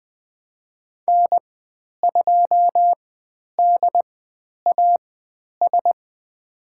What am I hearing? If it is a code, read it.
Morse code, heard as a keyed tone, about 20 wpm, spaced as for 10 wpm, 700 Hz. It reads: N2DAS
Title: 109 Lesson 18g - Letter m - Call Signs 20wpm